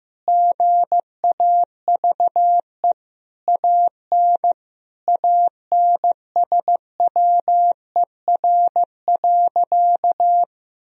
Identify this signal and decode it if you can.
Morse code, heard as a keyed tone, about 15 wpm, 700 Hz. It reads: GAVE AN ANSWER.
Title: Sentences from Top 1000 Words - Review Part 4 - 15wpm